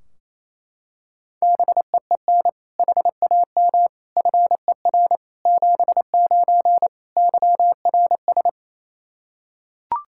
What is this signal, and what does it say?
Morse code, heard as a keyed tone, about 27 wpm, 700 Hz. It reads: BEEN HAM FER 79 YRS
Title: Practice QSO - Exchange 3 - More Info - 28wpm